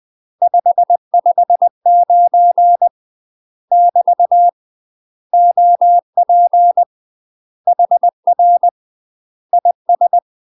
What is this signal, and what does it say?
Morse code, heard as a keyed tone, about 20 wpm, 700 Hz. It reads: 559 <BT> OP HR IS
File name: UK MASTER QSOs TRAINING/20WPM